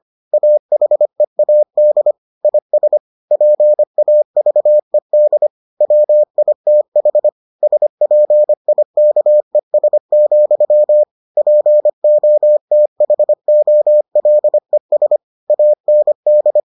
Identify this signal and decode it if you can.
Morse code, heard as a keyed tone, about 25 wpm, 600 Hz. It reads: AHEAD IS PAVED WITH SPIKES, POTHOLES AND